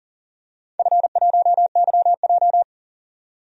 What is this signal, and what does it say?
Morse code, heard as a keyed tone, about 40 wpm, 700 Hz. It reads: F1YJ
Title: DXCC-100 Call Signs - 4 Characters - 40wpm